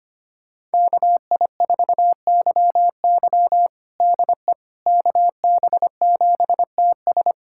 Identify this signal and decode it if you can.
Morse code, heard as a keyed tone, about 25 wpm, 700 Hz. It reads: KI4YY DE KB7TS
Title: USA Callsign DE Callsign - 5 Characters - 25wpm